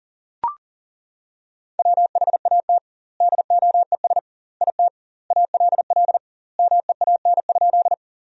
Morse code, heard as a keyed tone, about 40 wpm, 700 Hz. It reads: WHAT DOES IT ALL MEAN?
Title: 2x Word-Spacing - Sentences from Top 200 Words - 40wpm